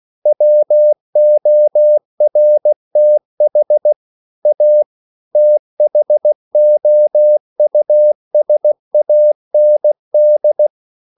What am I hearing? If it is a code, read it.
Morse code, heard as a keyed tone, about 16 wpm, 600 Hz. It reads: WORTH A THOUSAND
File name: CW / Practice / EnglishSayings16